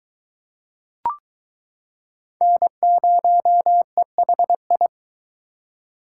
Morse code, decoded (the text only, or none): N0EHI